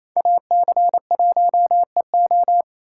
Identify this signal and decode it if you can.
Morse code, heard as a keyed tone, about 32 wpm, 700 Hz. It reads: AC1EO
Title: Speed Racing - USA 2x2 Call Signs 28wpm